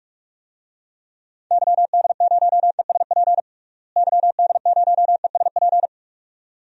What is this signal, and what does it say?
Morse code, heard as a keyed tone, about 46 wpm, 700 Hz. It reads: YD0ESP YD0ESP
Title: DXCC-100 Call Signs - 6 Characters - Repeated 2x - 45wpm